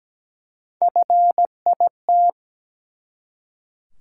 Morse code, heard as a keyed tone, about 17 wpm, 700 Hz. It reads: FIT
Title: Three Letter Words - 17wpm